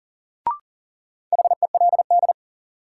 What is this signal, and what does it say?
Morse code, heard as a keyed tone, about 40 wpm, 700 Hz. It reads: HELD
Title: Sentences from Top 1600 Words - New Word Drill - 40wpm